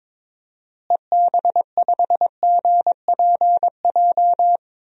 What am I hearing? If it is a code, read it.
Morse code, heard as a keyed tone, about 22 wpm, 700 Hz. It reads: EB5GPJ
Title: DXCC-100 Call Signs - 6 Characters - Repeated 2x - 22wpm